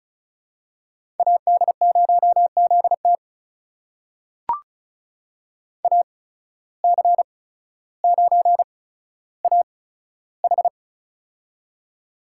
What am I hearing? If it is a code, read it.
Morse code, heard as a keyed tone, about 34 wpm, spaced as for 15 wpm, 700 Hz. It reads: AD0ZT AC9AH
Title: USA Call Signs - 5 Characters - 8x Character Spacing - 35wpm